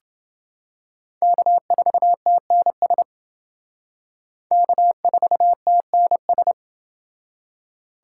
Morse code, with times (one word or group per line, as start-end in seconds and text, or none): 1.2-3.0 K4TNS
4.5-6.5 K4TNS